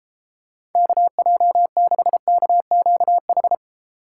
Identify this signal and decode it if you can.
Morse code, heard as a keyed tone, about 30 wpm, 700 Hz. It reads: KJ6KQH